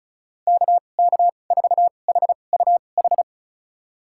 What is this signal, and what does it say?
Morse code, heard as a keyed tone, about 34 wpm, spaced as for 25 wpm, 700 Hz. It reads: KK4HUH
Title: USA Call Signs - 6 Characters - 2x Character Spacing - 35wpm